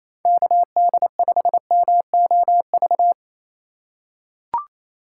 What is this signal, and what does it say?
Morse code, heard as a keyed tone, about 28 wpm, 700 Hz. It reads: KD5MOV
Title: USA Call Signs - 4x Character Spacing - 28wpm